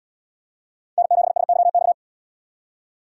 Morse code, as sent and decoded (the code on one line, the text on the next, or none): -. --... ... -.-- -..-
N7SYX